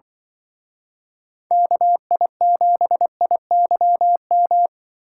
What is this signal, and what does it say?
Morse code, heard as a keyed tone, about 24 wpm, 700 Hz. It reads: KI7IYM
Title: Speed Racing - USA Call Signs 20wpm